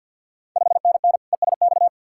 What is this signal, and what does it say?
Morse code, heard as a keyed tone, about 53 wpm, 690 Hz. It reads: 5NN ESX
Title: Ontario QSO Party Exchange 50wpm